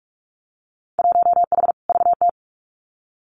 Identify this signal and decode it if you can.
Morse code, heard as a keyed tone, about 46 wpm, 700 Hz. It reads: 1H VT
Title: Winter Field Day Exchanges 45wpm